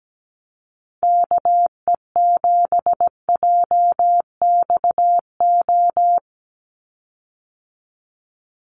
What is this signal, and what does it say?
Morse code, heard as a keyed tone, about 17 wpm, 700 Hz. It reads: KE7JXO
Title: USA Call Signs 17wpm